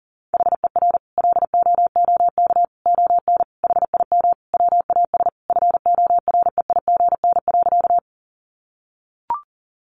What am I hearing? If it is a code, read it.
Morse code, heard as a keyed tone, about 40 wpm, 700 Hz. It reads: HER LOOK ON HIM WAS FOREIGN.
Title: Sentences from Top 1400 Words 40wpm